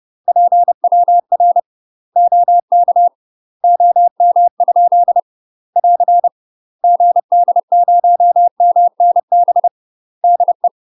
Morse code, heard as a keyed tone, about 30 wpm, 700 Hz. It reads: PWR OK OM? <AR> GD0MNB DE
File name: UK MASTER QSOs TRAINING / 30WPM / UK MASTER QSOs 0017